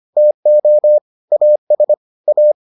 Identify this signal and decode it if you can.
Morse code, heard as a keyed tone, about 25 wpm, 600 Hz. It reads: TO AS A